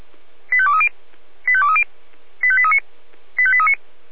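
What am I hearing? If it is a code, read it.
Five-tone selective call (ZVEI), heard as quick runs of five tones, tones of about 70 ms. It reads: 85230 85230 85839 85839
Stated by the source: ZVEI five-tone call 85855 received by SDR.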